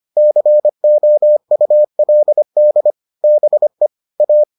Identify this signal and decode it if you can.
Morse code, heard as a keyed tone, about 25 wpm, 600 Hz. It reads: COULD BE A